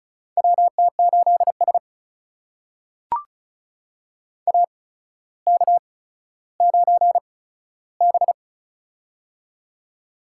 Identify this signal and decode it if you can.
Morse code, heard as a keyed tone, about 36 wpm, spaced as for 15 wpm, 700 Hz. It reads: WT8S AK9B
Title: USA Call Signs - 4 Characters - 8x Character Spacing - 35wpm